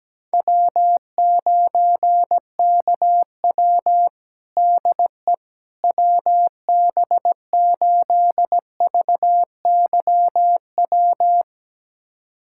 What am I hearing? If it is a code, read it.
Morse code, heard as a keyed tone, about 17 wpm, 700 Hz. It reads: W9KW DE WB8VYW
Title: USA Callsign DE Callsign 17wpm